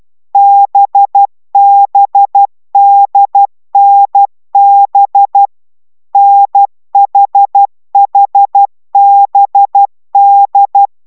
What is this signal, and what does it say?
Morse code, heard as a keyed tone, about 12 wpm, 800 Hz. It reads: BBDNB NHHBD